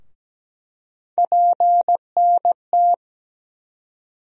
Morse code, heard as a keyed tone, about 17 wpm, 700 Hz. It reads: PNT
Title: USA Call Sign Postfixes - 3 Letters 17wpm